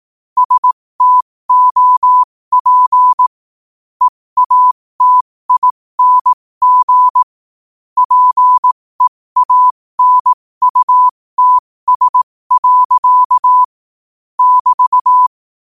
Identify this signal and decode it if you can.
Morse code, heard as a keyed tone, about 18 wpm, spaced as for 15 wpm, 1000 Hz. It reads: STOP EATING PEANUTS. <BT>
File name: qotdfiles / 15wpm